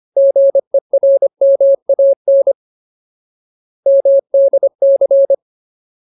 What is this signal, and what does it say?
Morse code, heard as a keyed tone, about 25 wpm, 550 Hz. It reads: GERMAN MDC
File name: Ultimate Word List / 25WPM Ultimate